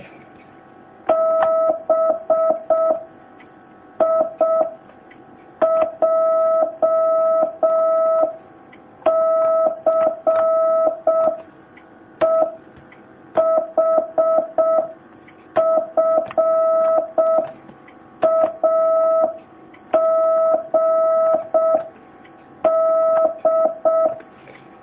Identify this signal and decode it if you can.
Morse code, heard as a keyed tone, about 6 wpm, 650 Hz. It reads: B I J C E H F A G D
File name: jh8cbh / 2019cw